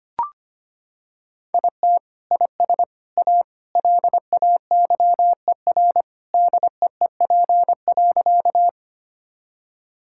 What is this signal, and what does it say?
Morse code, heard as a keyed tone, about 25 wpm, 700 Hz. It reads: IT IS A LAYER DEEP.
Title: Sentences from Top 1600 Words 25wpm